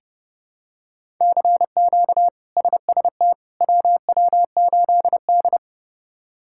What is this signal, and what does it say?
Morse code, heard as a keyed tone, about 30 wpm, 700 Hz. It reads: CQ SST WW8D
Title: K1USN SST QSO Practice - 30wpm